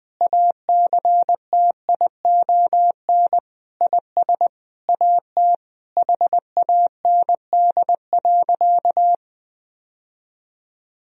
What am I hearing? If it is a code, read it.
Morse code, heard as a keyed tone, about 19 wpm, 700 Hz. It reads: ACTION IS AT HAND.
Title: Sentences from Top 1000 Words - Review Part 4 - 20wpm